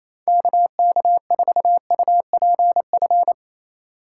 Morse code, heard as a keyed tone, about 28 wpm, 700 Hz. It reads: KK4UPF
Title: USA Call Signs - 2x Character Spacing - 28wpm